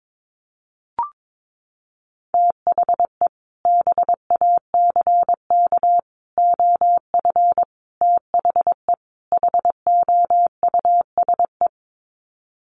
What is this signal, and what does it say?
Morse code, heard as a keyed tone, about 22 wpm, 700 Hz. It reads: THE BACK OF THE HOUSE
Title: Sets of 5 Words Encore2 22wpm